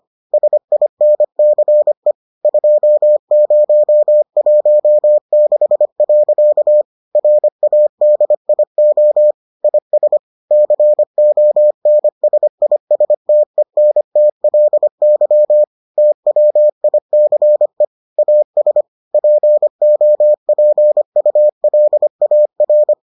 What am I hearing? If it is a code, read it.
Morse code, heard as a keyed tone, about 25 wpm, 600 Hz. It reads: SINCE 2016. RADIO IS CONSISTENTLY TWICE AS POPULAR